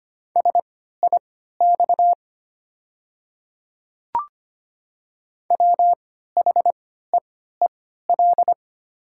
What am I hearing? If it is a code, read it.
Morse code, heard as a keyed tone, about 25 wpm, spaced as for 14 wpm, 690 Hz. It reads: SIX WHEEL
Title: 3x Character Spacing - Top 500 Words - 25wpm